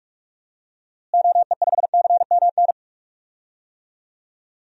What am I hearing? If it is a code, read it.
Morse code, heard as a keyed tone, about 46 wpm, 700 Hz. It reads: OE5CMN
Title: DXCC-100 Call Signs - 6 Characters - Repeated 2x - 45wpm